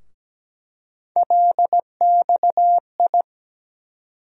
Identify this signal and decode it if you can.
Morse code, heard as a keyed tone, about 17 wpm, 700 Hz. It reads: LXI
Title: USA Call Sign Postfixes - 3 Letters 17wpm